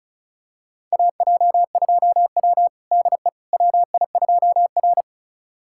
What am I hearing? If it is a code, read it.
Morse code, heard as a keyed tone, about 36 wpm, 700 Hz. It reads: AJ2W DE WI2R
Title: USA Callsign DE Callsign - 4 Characters - 35wpm